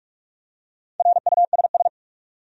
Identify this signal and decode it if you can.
Morse code, heard as a keyed tone, about 46 wpm, 700 Hz. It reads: RUSS
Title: USA Common Names - Max 4 Letters - 45wpm